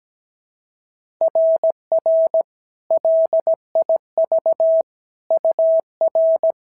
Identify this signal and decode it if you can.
Morse code, heard as a keyed tone, about 16 wpm, 650 Hz. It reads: RR LIV UR
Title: SOTA QSO Practice 17wpm